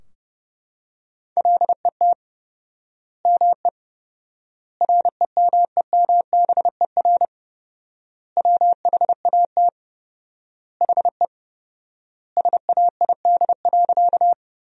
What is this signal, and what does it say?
Morse code, heard as a keyed tone, about 30 wpm, 700 Hz. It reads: LET ME REMEMBER WHAT HE SAID.
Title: 4x Word-Spacing - Sentences from Top 400 Words - 30wpm